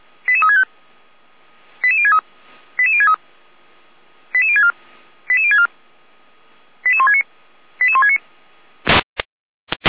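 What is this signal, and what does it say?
Five-tone selective call (ZVEI), heard as quick runs of five tones, tones of about 80 ms. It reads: 88365 88063 88063 88064 88064 88169 88169
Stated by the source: ZVEI five-tone selective call 88352, received via SDR on channel 455.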